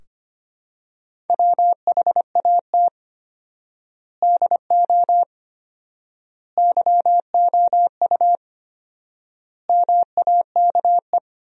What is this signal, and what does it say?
Morse code, heard as a keyed tone, about 25 wpm, 700 Hz. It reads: WHAT DO YOU MAKE
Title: 4x Word-Spacing - Sentences from Top 100 Words - 25wpm